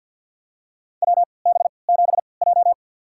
Morse code, heard as a keyed tone, about 46 wpm, spaced as for 27 wpm, 700 Hz. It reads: WB7J
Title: USA Call Signs - 4 Characters - 3x Character Spacing - 50wpm